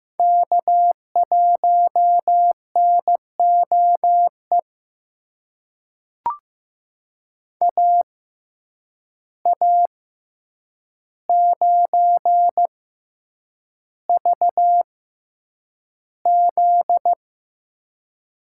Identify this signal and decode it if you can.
Morse code, heard as a keyed tone, about 15 wpm, spaced as for 8 wpm, 700 Hz. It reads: K1NOE AA9VZ
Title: USA Call Signs - 6x Character Spacing - 15wpm